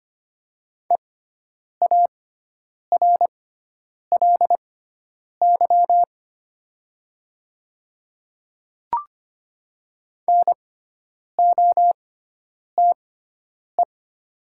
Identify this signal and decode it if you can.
Morse code, heard as a keyed tone, about 25 wpm, spaced as for 9 wpm, 700 Hz. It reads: EARLY NOTE
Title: 6x Character Spacing - Top 500 Words - 25wpm